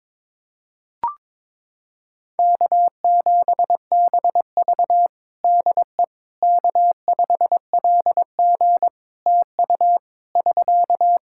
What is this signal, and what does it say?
Morse code, heard as a keyed tone, about 22 wpm, 700 Hz. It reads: K7BV DE K5LG TU <SK>
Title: K1USN SST QSO Practice 22wpm